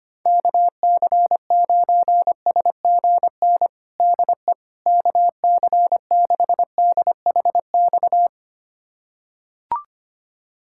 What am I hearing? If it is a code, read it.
Morse code, heard as a keyed tone, about 25 wpm, 700 Hz. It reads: KC9SGN DE KC6DHX
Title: USA Callsign DE Callsign 25wpm